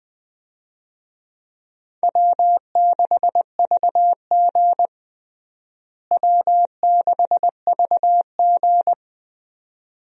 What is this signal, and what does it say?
Morse code, heard as a keyed tone, about 20 wpm, 700 Hz. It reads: W6VG W6VG
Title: USA Call Signs - 4 Characters - Repeated 2x - 20wpm